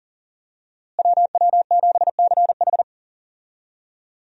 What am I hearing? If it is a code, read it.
Morse code, heard as a keyed tone, about 40 wpm, 700 Hz. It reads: WW7CH
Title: USA Call Signs - 5 Characters - Repeated 2x - 40wpm